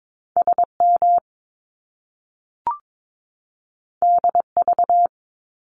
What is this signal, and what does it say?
Morse code, heard as a keyed tone, about 22 wpm, 700 Hz. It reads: SM DV
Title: Top Two Letter Combinations in Words 22wpm